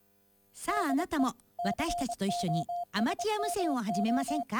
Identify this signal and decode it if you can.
Morse code, heard as a keyed tone, about 25 wpm, 700 Hz, with speech. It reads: DE COSMO